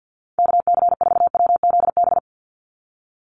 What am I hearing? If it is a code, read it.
Morse code, heard as a keyed tone, about 50 wpm, 700 Hz. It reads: KC4WZB
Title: NA SOTA Activator Callsigns 50wpm